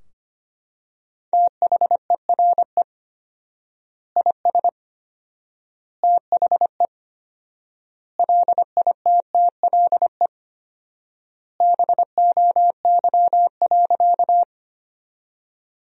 Morse code, heard as a keyed tone, about 25 wpm, 700 Hz. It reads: THERE IS THE LITTLE BOY.
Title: 4x Word-Spacing - Sentences from Top 200 Words - 25wpm